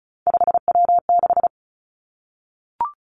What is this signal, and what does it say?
Morse code, heard as a keyed tone, about 36 wpm, 700 Hz. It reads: 5W6
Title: Three Challenging Letter-Number Combinations 35wpm